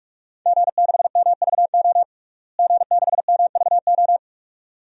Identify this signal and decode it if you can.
Morse code, heard as a keyed tone, about 46 wpm, 700 Hz. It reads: G6MVO G6MVO
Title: DXCC-100 Call Signs - 5 Characters - Repeated 2x - 45wpm